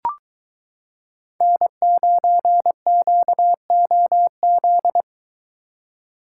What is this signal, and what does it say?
Morse code, heard as a keyed tone, about 22 wpm, 700 Hz. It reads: N9QOZ